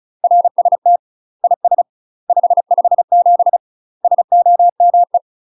Morse code, heard as a keyed tone, about 35 wpm, 700 Hz. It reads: RST IS 557 SOME
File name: UK MASTER QSOs TRAINING / 35WPM / UK MASTER QSOs 0010